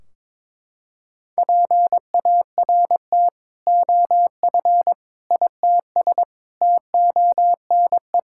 Morse code, heard as a keyed tone, about 22 wpm, 700 Hz. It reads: PART OF ITS TONE
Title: Sentences from Top 1600 Words 22wpm